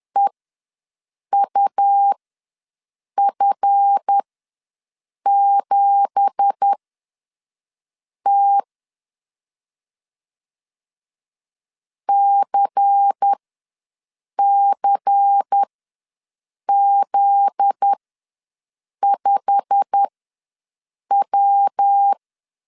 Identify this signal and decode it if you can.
Morse code, heard as a keyed tone, about 10 wpm, spaced as for 6 wpm, 800 Hz. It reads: EUF7T CCZ5W
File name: LEZIONI / 9 2867 ws 10wpm 4215